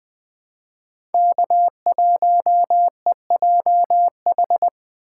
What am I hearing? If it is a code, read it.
Morse code, heard as a keyed tone, about 20 wpm, 700 Hz. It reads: K1EJH
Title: USA Call Signs - 2x Character Spacing - 20wpm